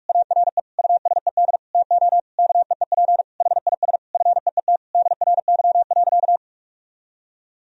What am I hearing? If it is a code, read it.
Morse code, heard as a keyed tone, about 42 wpm, 700 Hz. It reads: ARE USED TO KEEP HIS FEET DRY.
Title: Sentences from Top 1000 Words - Review Part 3 - 45wpm